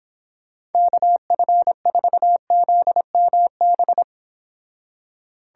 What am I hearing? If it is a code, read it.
Morse code, heard as a keyed tone, about 30 wpm, 700 Hz. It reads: KF4ZMB